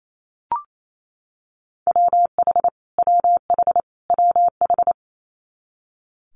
Morse code, heard as a keyed tone, about 28 wpm, 700 Hz. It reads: WH WH WH
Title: Multi-letter Phonemes Repeated 3x 28wpm